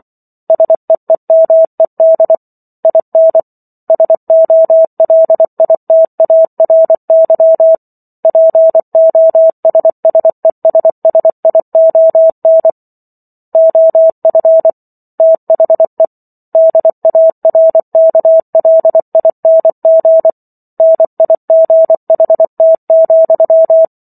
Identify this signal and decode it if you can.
Morse code, heard as a keyed tone, about 24 wpm, 650 Hz. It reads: SEEMED IN SOLITARY POSSESSION OF THE DARKLING NIGHT,